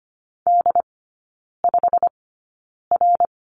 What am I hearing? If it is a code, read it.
Morse code, heard as a keyed tone, about 25 wpm, spaced as for 9 wpm, 700 Hz. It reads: D5R
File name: Lesson 15e - Number 5 - Three Characters 25wpm